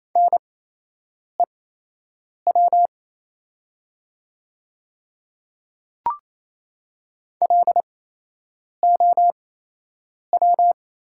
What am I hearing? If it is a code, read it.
Morse code, heard as a keyed tone, about 28 wpm, spaced as for 8 wpm, 700 Hz. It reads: NEW LOW
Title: 8x Character Spacing - Top 300 Words - 28wpm